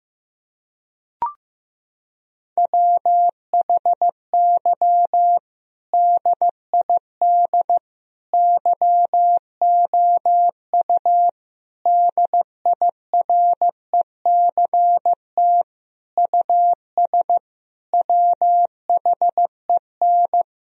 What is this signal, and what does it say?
Morse code, heard as a keyed tone, about 15 wpm, 700 Hz. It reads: WHY DID YOU DIRECT US WHEN
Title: Sentences from Top 1000 Words - Review Part 2 - 15wpm